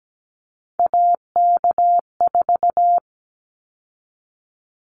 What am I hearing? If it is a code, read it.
Morse code, heard as a keyed tone, about 17 wpm, 700 Hz. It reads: AK4
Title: Callsign Prefixes for Top 25 DXCC - 17wpm